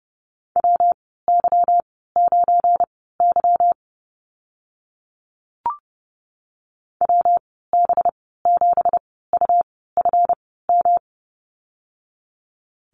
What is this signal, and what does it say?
Morse code, heard as a keyed tone, about 30 wpm, spaced as for 17 wpm, 700 Hz. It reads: WY9Y WB7UFM
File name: Lesson 37g - Letter x - Call Signs 30wpm